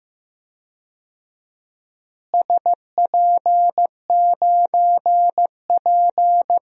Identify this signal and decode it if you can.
Morse code, heard as a keyed tone, about 15 wpm, 700 Hz. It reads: SP9P
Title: DXCC-100 Call Signs - 4 Characters - Repeated 2x - 15wpm